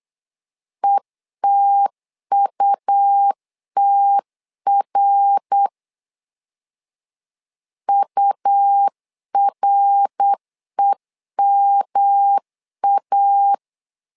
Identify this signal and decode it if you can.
Morse code, heard as a keyed tone, about 8 wpm, 800 Hz. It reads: ETUTR UREMA
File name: LEZIONI / 3 MWRO / 3 MWR 8wpm 2444